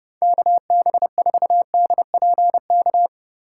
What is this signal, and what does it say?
Morse code, heard as a keyed tone, about 30 wpm, 700 Hz. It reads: KB4DPK
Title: USA Call Signs - 6 Characters - 2x Character Spacing - 30wpm